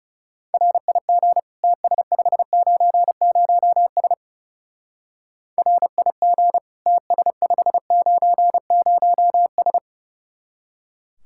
Morse code, heard as a keyed tone, about 38 wpm, 700 Hz. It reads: RIG TS590S RIG TS590S